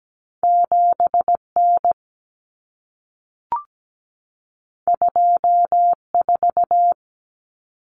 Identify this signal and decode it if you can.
Morse code, heard as a keyed tone, about 17 wpm, 700 Hz. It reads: 7N 24
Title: Common Cut Numbers - 2-Digit 17wpm